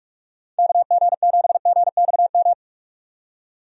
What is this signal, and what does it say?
Morse code, heard as a keyed tone, about 43 wpm, 700 Hz. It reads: KG7GXM
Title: USA Call Signs - 6 Characters - 45wpm